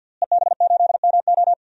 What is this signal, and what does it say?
Morse code, heard as a keyed tone, about 49 wpm, 700 Hz. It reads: EB8MO
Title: DXCC-100 Call Signs - 5 Characters - Repeated 2x - 50wpm